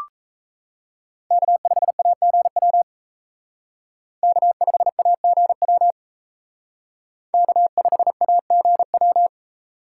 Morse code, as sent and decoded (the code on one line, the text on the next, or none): -.- ..... .- --. .-- / -.- ..... .- --. .-- / -.- ..... .- --. .--
K5AGW K5AGW K5AGW